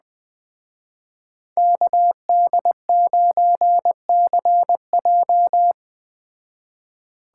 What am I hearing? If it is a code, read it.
Morse code, heard as a keyed tone, about 20 wpm, 700 Hz. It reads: KD9CJ